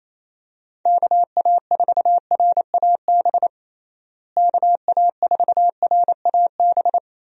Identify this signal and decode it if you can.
Morse code, heard as a keyed tone, about 28 wpm, 700 Hz. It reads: KA4RAB KA4RAB
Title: USA Call Signs - 6 Characters - Repeated 2x - 28wpm